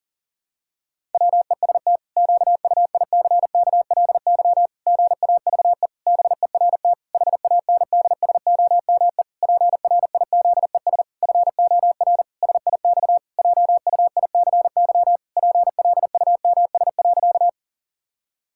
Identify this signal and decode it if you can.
Morse code, heard as a keyed tone, about 40 wpm, 700 Hz. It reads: WEST QUICKLY GAVE BERT HANDSOME PRIZES FOR SIX JUICY PLUMS.
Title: Pangram Challenge v2 - 40wpm